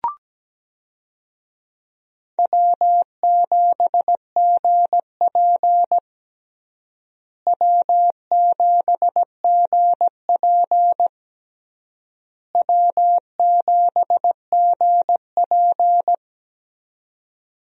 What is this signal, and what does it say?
Morse code, heard as a keyed tone, about 17 wpm, 700 Hz. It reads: W7GP W7GP W7GP
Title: USA Call Signs - 4 Characters - Repeated 3x - 17wpm